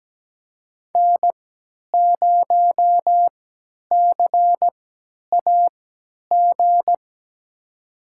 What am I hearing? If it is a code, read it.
Morse code, heard as a keyed tone, about 17 wpm, spaced as for 10 wpm, 700 Hz. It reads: N0CAG
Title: USA Call Signs - 3x Character Spacing - 17wpm